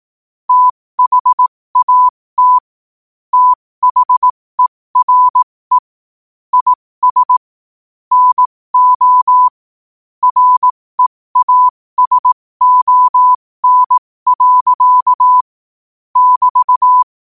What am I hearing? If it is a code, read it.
Morse code, heard as a keyed tone, about 18 wpm, spaced as for 15 wpm, 1000 Hz. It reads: THAT THERE IS NO REASON. <BT>
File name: qotdfiles / 15wpm